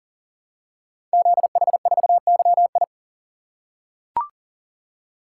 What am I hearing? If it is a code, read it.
Morse code, heard as a keyed tone, about 38 wpm, 690 Hz. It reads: ZH4YI
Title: Code Groups - 5 Letter:Numbers - 40wpm